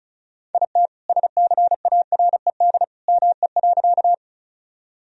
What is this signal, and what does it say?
Morse code, heard as a keyed tone, about 34 wpm, 700 Hz. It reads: IT SCARED ME.
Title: Sentences from Top 1000 Words - Review Part 4 - 35wpm